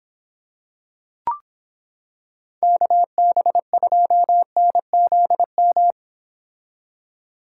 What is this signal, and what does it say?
Morse code, heard as a keyed tone, about 26 wpm, 700 Hz. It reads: KB2NZM